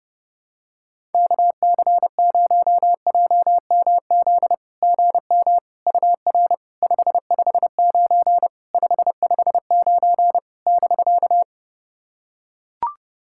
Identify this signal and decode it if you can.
Morse code, heard as a keyed tone, about 30 wpm, 690 Hz. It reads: KC0JMZ GM UR 559 559 <BK>
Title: SOTA QSO Practice 30wpm